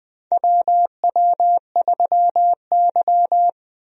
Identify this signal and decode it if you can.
Morse code, heard as a keyed tone, about 21 wpm, 700 Hz. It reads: WW3Y
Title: USA Call Signs - 4 Characters - Repeated 3x - 20wpm